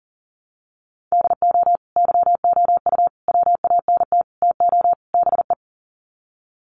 Morse code, heard as a keyed tone, about 40 wpm, 700 Hz. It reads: DO YOU WANT TO BE
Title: Sets of 5 Words Encore2 40wpm